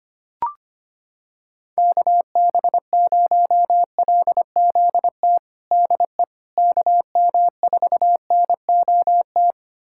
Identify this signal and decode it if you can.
Morse code, heard as a keyed tone, about 25 wpm, 700 Hz. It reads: KB0LZT DE KM4NOT
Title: USA Callsign DE Callsign - 6 Characters - 25wpm